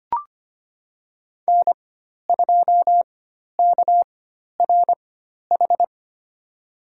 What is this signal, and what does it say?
Morse code, heard as a keyed tone, about 25 wpm, spaced as for 12 wpm, 700 Hz. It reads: N2KRH
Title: USA Call Signs - 4x Character Spacing - 25wpm